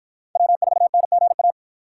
Morse code, heard as a keyed tone, about 50 wpm, 700 Hz. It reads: W4NGA